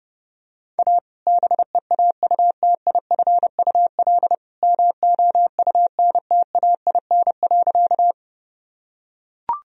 Morse code, heard as a keyed tone, about 32 wpm, 700 Hz. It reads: A BEAUTIFUL MOUNTAIN.
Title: Sentences from Top 1000 Words - Review Part 1 - 30wpm